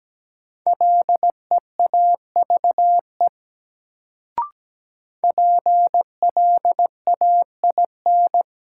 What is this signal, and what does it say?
Morse code, heard as a keyed tone, about 17 wpm, 700 Hz. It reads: LEAVE PLAIN
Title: Sentences from Top 400 Words - New Word Drill - 17wpm